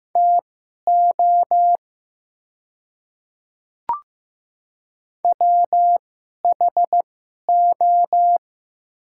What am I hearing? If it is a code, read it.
Morse code, heard as a keyed tone, about 15 wpm, spaced as for 11 wpm, 700 Hz. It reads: TO WHO
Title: 2x Character Spacing - Top 100 Words - 15wpm